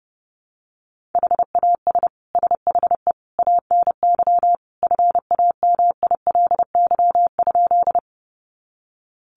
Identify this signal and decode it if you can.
Morse code, heard as a keyed tone, about 30 wpm, 700 Hz. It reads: HAS SHE ANY FAMILY?
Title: Sentences from Top 400 Words 30wpm